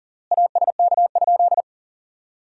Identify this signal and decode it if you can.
Morse code, heard as a keyed tone, about 37 wpm, 700 Hz. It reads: ASK?
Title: 3x Word-Spacing - Sentences from Top 200 Words - 40wpm